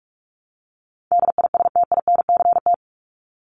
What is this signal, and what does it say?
Morse code, heard as a keyed tone, about 44 wpm, 700 Hz. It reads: DISTINCT
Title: Top 1501-1750 Words 45wpm